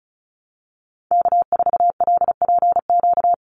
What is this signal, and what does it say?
Morse code, heard as a keyed tone, about 36 wpm, 700 Hz. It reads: K4LPQ
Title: NA SOTA Activator Callsigns 35wpm